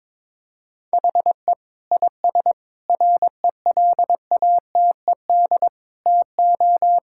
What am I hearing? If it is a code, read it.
Morse code, heard as a keyed tone, about 22 wpm, 700 Hz. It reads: HE IS RELATED TO
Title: Sentences from Top 1000 Words - Review Part 5 - 22wpm